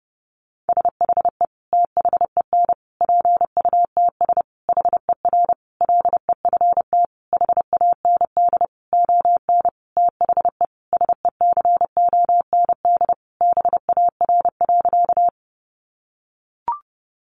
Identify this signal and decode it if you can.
Morse code, heard as a keyed tone, about 28 wpm, 700 Hz. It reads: SHE THEN PUTS HER LEFT HAND ON THE SECOND BAR.
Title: Easy Phrases 30wpm